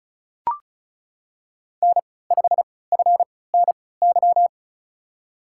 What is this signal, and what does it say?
Morse code, heard as a keyed tone, about 36 wpm, spaced as for 20 wpm, 700 Hz. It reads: N5FNY
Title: USA Call Signs - 5 Characters - 3x Character Spacing - 35wpm